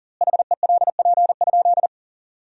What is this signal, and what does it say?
Morse code, heard as a keyed tone, about 40 wpm, 700 Hz. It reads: HELP?
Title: 2x Word-Spacing - Sentences from Top 200 Words - 40wpm